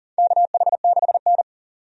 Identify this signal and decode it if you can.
Morse code, heard as a keyed tone, about 38 wpm, 700 Hz. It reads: KH6N
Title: DXCC-100 Call Signs - 4 Characters - 40wpm